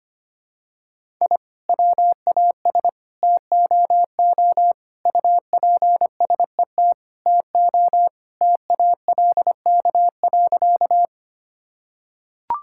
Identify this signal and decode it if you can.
Morse code, heard as a keyed tone, about 25 wpm, 700 Hz. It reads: I WAS TOO UPSET TO TALK.
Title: Sentences from Top 1000 Words - Review Part 5 - 25wpm